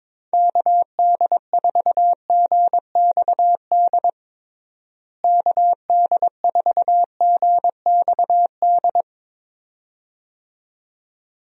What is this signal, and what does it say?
Morse code, heard as a keyed tone, about 22 wpm, 700 Hz. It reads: KD4GXD KD4GXD
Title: USA Call Signs - 6 Characters - Repeated 2x - 22wpm